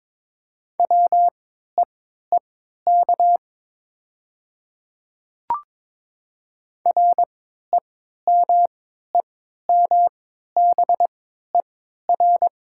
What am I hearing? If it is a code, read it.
Morse code, heard as a keyed tone, about 21 wpm, spaced as for 12 wpm, 690 Hz. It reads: WEEK REMEMBER
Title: 3x Character Spacing - Top 500 Words - 22wpm